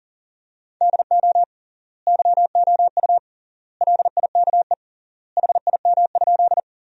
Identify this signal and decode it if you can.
Morse code, heard as a keyed tone, about 37 wpm, 700 Hz. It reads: DO YOU LIKE HIM?
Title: 3x Word-Spacing - Sentences from Top 100 Words - 40wpm